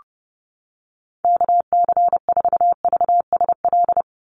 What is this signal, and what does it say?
Morse code, heard as a keyed tone, about 28 wpm, 700 Hz. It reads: KC4VSL
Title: USA Call Signs 30wpm